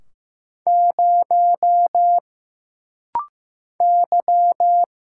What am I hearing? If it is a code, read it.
Morse code, heard as a keyed tone, about 15 wpm, 700 Hz. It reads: 0 Y
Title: Single Letter-Number Alt - Rapid-Fire 15wpm